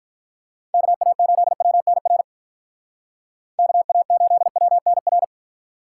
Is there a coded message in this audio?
Yes: Morse code from 0.7 to 5.2 s, heard as a keyed tone, about 45 wpm, 700 Hz. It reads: KA8WNR KA8WNR